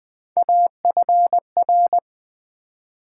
Morse code, heard as a keyed tone, about 20 wpm, 700 Hz. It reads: AFR
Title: Build A Word - 3 Letters At A Time - 6-Letter Words - 20wpm